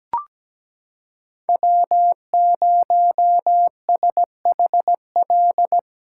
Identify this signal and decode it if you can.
Morse code, heard as a keyed tone, about 17 wpm, 700 Hz. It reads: W0SHL
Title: K1USN SST QSO Practice - 17wpm